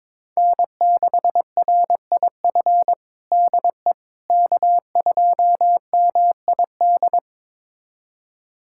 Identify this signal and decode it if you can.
Morse code, heard as a keyed tone, about 22 wpm, 700 Hz. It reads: N6RIF DE K2MID
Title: USA Callsign DE Callsign - 5 Characters - 22wpm